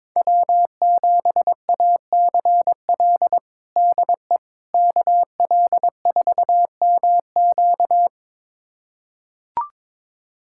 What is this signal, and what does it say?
Morse code, heard as a keyed tone, about 22 wpm, 700 Hz. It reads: W7ACL DE KL4MQ
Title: USA Callsign DE Callsign - 5 Characters - 22wpm